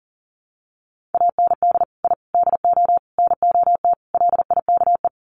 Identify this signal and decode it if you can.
Morse code, heard as a keyed tone, about 40 wpm, 700 Hz. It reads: AND I DO NOT LIKE
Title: Sets of 5 Words Encore2 40wpm